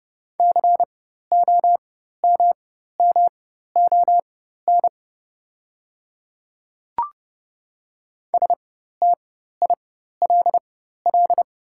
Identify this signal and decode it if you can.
Morse code, heard as a keyed tone, about 30 wpm, spaced as for 14 wpm, 700 Hz. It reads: COMMON STILL
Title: 4x Character Spacing - Top 500 Words - 30wpm